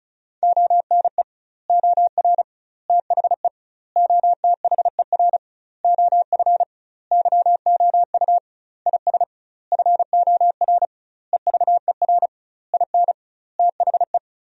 Morse code, heard as a keyed tone, about 36 wpm, 700 Hz. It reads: ONE OR THE OTHER OF YOU IS FOR EVER IN THE
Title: 2x Word-Spacing - Sentences from Top 400 Words - 35wpm